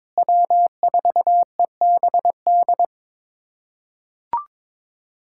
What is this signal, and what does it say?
Morse code, heard as a keyed tone, about 22 wpm, 700 Hz. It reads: W4EBD
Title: USA Callsign DE Callsign 22wpm